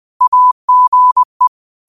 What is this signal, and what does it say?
Morse code, heard as a keyed tone, about 20 wpm, 1000 Hz. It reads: AGE
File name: qotdfiles / 20wpm